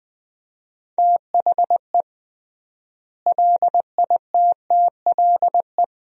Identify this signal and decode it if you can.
Morse code, heard as a keyed tone, about 20 wpm, 700 Hz. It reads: THE LITTLE
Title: 3x Word-Spacing - Sentences from Top 500 Words - 20wpm